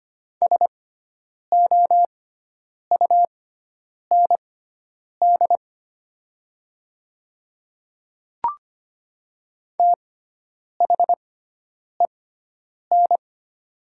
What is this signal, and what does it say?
Morse code, heard as a keyed tone, about 25 wpm, spaced as for 9 wpm, 700 Hz. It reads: SOUND THEN
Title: 6x Character Spacing - Top 200 Words - 25wpm